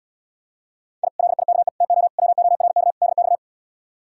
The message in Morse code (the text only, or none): I LIKE TO WORK DX